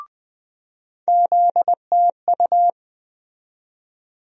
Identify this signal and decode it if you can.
Morse code, heard as a keyed tone, about 20 wpm, 700 Hz. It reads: ZTU
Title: USA Call Sign Postfixes - 3 Letters 20wpm